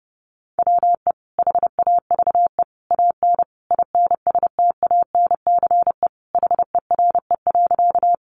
Morse code, heard as a keyed tone, about 30 wpm, 700 Hz. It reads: WE HAVE AN INSTANCE HERE.
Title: Sentences from Top 800 Words 30wpm